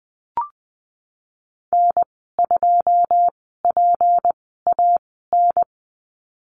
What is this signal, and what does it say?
Morse code, heard as a keyed tone, about 20 wpm, spaced as for 15 wpm, 700 Hz. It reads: N2PAN
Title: 167 Lesson 27g - Number 7 - Call Signs 20wpm